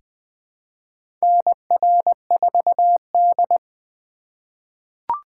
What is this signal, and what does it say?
Morse code, heard as a keyed tone, about 21 wpm, 700 Hz. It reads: NR4D
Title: USA Call Signs - 4 Characters - 4x Character Spacing - 20wpm